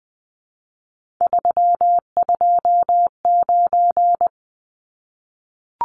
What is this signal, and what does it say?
Morse code, heard as a keyed tone, about 20 wpm, 700 Hz. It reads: 329
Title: Numbers - 3-Digit 20wpm